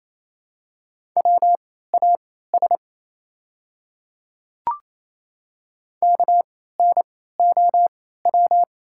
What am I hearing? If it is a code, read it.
Morse code, heard as a keyed tone, about 29 wpm, spaced as for 16 wpm, 690 Hz. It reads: WAS KNOW
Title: 3x Character Spacing - Top 100 Words - 28wpm